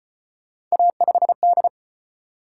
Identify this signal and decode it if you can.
Morse code, heard as a keyed tone, about 35 wpm, 700 Hz. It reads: A5D